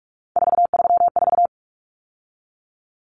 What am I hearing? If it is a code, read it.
Morse code, heard as a keyed tone, about 44 wpm, 700 Hz. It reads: 434
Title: Common Cut Numbers - 3-Digit 45wpm